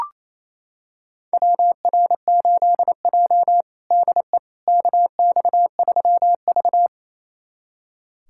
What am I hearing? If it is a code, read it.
Morse code, heard as a keyed tone, about 28 wpm, 700 Hz. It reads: WR8J DE KX3V
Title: USA Callsign DE Callsign - 4 Characters - 28wpm